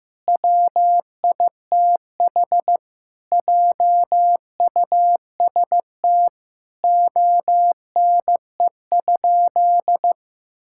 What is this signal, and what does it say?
Morse code, heard as a keyed tone, about 15 wpm, 700 Hz. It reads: WITH JUST ONE?
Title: Sentences from Top 1000 Words - Review Part 6 - 15wpm